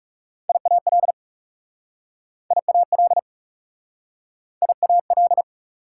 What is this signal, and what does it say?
Morse code, heard as a keyed tone, about 37 wpm, 700 Hz. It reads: IAL IAL IAL
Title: Speed-Racing Prefixes and Suffixes 30wpm